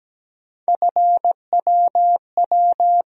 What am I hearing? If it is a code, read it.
Morse code, heard as a keyed tone, about 17 wpm, 700 Hz. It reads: FWW
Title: USA Call Sign Postfixes - 3 Letters 17wpm